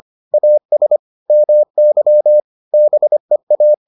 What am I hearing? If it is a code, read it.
Morse code, heard as a keyed tone, about 25 wpm, 600 Hz. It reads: AS MY BEA